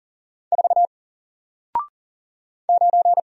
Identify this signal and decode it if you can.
Morse code, heard as a keyed tone, about 39 wpm, 700 Hz. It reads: 4 9
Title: Single Letter-Number Alt 40wpm